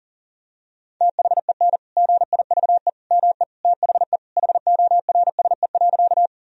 Morse code, heard as a keyed tone, about 37 wpm, 700 Hz. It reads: THEN GIVE ME THE HORSE.
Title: Sentences from Top 1000 Words - Review Part 1 - 40wpm